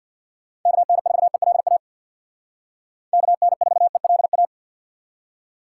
Morse code, heard as a keyed tone, about 65 wpm, 700 Hz. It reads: KN4ELA KN4ELA